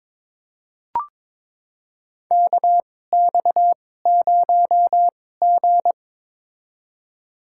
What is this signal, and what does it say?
Morse code, heard as a keyed tone, about 22 wpm, spaced as for 16 wpm, 700 Hz. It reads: KX0G
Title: USA Call Signs - 2x Character Spacing - 22wpm